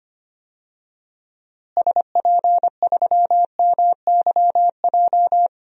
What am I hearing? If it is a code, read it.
Morse code, heard as a keyed tone, about 25 wpm, 700 Hz. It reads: SP3MYJ
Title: DXCC-100 Call Signs - 6 Characters - Repeated 3x - 25wpm